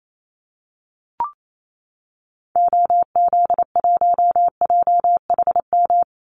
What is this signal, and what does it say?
Morse code, heard as a keyed tone, about 28 wpm, 700 Hz. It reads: OZ1JHM
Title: Call Signs - 28wpm